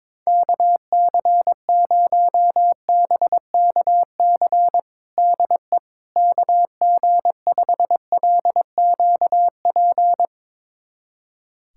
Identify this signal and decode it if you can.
Morse code, heard as a keyed tone, about 22 wpm, 700 Hz. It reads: KC0BKC DE KG5LQP
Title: USA Callsign DE Callsign - 6 Characters - 22wpm